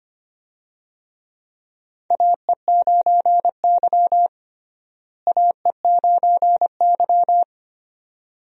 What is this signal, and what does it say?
Morse code, heard as a keyed tone, about 25 wpm, 700 Hz. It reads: AE9Y AE9Y
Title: USA Call Signs - 4 Characters - Repeated 2x - 25wpm